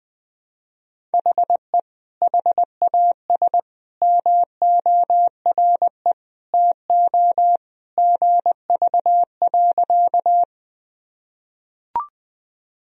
Morse code, heard as a keyed tone, about 21 wpm, 700 Hz. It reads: HE HAS MORE TO GV.
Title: Sentences from Top 200 Words and Abbreviations 20wpm